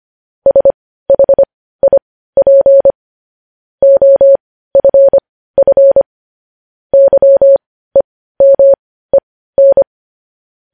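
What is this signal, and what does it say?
Morse code, heard as a keyed tone, about 25 wpm, spaced as for 15 wpm, 550 Hz. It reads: SHIP OFF YEMEN
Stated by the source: Morse code news headlines sent at 15 wpm.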